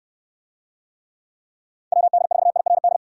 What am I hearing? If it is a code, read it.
Morse code, heard as a keyed tone, about 69 wpm, 700 Hz. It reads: WD4IRD